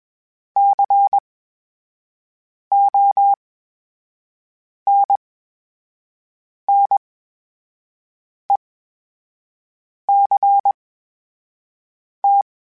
Morse code, heard as a keyed tone, about 21 wpm, spaced as for 5 wpm, 800 Hz. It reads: CONNECT
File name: Special INGs-mp3s / Special INGs-mp3s / VerySlowToFaster Ing @20